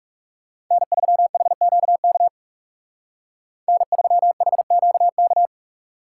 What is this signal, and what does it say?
Morse code, heard as a keyed tone, about 39 wpm, 700 Hz. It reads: N3HQK N3HQK